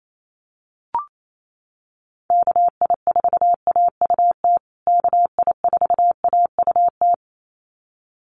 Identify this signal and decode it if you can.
Morse code, heard as a keyed tone, about 28 wpm, 700 Hz. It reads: KI4AUT KI4AUT
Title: USA Call Signs Repeated 2x 28wpm